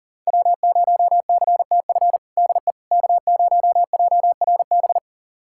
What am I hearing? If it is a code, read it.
Morse code, heard as a keyed tone, about 37 wpm, 700 Hz. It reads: W0CTF DE K0JRB
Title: USA Callsign DE Callsign - 5 Characters - 40wpm